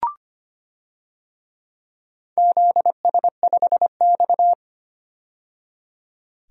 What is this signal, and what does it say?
Morse code, heard as a keyed tone, about 25 wpm, 700 Hz. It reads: ZS5X
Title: DXCC-100 Call Signs - 4 Characters - 25wpm